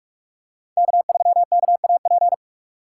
Morse code, heard as a keyed tone, about 48 wpm, 700 Hz. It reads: K3KAP
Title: USA Call Signs - 5 Characters - 3x Character Spacing - 45wpm